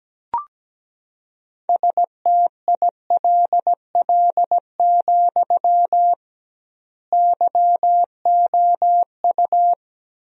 Morse code, heard as a keyed tone, about 17 wpm, 700 Hz. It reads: STILL, YOU
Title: 2x Word-Spacing - Sentences from Top 300 Words - 17wpm